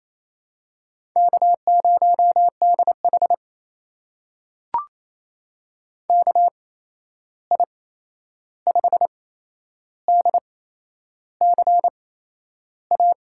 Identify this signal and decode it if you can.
Morse code, heard as a keyed tone, about 28 wpm, spaced as for 11 wpm, 700 Hz. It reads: K0DH KI5DCA
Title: USA Call Signs - 8x Character Spacing - 28wpm